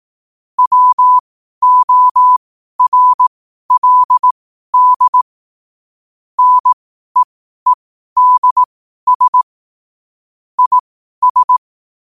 Morse code, heard as a keyed tone, about 18 wpm, spaced as for 12 wpm, 1000 Hz. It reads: WORLD NEEDS IS
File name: qotdfiles / 12wpm